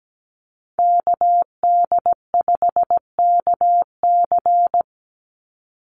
Morse code, heard as a keyed tone, about 17 wpm, 700 Hz. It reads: KD5KC
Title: NA SOTA Activator Callsigns 17wpm